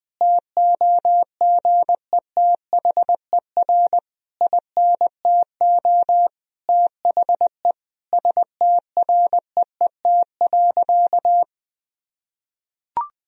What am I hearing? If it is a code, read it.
Morse code, heard as a keyed tone, about 20 wpm, 700 Hz. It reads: TOGETHER INTO THE STREET.
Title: Sentences from Top 1000 Words - Review Part 3 - 20wpm